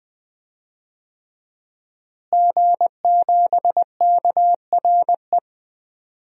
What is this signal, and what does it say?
Morse code, heard as a keyed tone, about 21 wpm, 700 Hz. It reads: G7KRE
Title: DXCC-100 Call Signs - 5 Characters - Repeated 3x - 20wpm